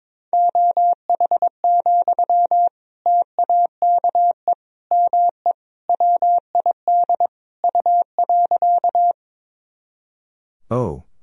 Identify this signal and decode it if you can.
Morse code, heard as a keyed tone, about 22 wpm, 700 Hz. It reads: OH, TAKE ME WID U.
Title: Sentences from Top 500 Words and Abbreviations 22wpm